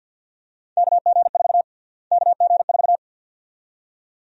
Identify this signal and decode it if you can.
Morse code, heard as a keyed tone, about 50 wpm, 700 Hz. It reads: KG4 KG4
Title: USA Call Sign Prefixes - Repeated 2x - 50wpm